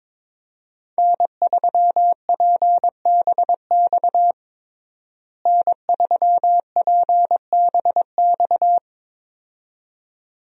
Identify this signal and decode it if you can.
Morse code, heard as a keyed tone, about 22 wpm, 700 Hz. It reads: N3PBX N3PBX
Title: USA Call Signs - 5 Characters - Repeated 2x - 22wpm